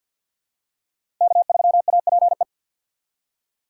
Morse code, heard as a keyed tone, about 53 wpm, 690 Hz. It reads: K3APE
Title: USA Call Signs - 5 Characters - 3x Character Spacing - 50wpm